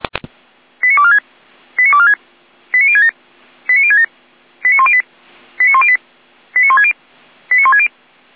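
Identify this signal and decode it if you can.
Five-tone selective call (ZVEI), heard as quick runs of five tones, tones of about 70 ms. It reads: 89246 89246 89067 89067 89108 89108 89160 89160
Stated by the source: ZVEI five-tone selective call received with an SDR.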